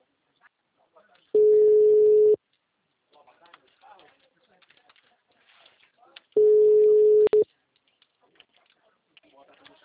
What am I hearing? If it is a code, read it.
Ringback tone (Europe, 425 Hz), with speech.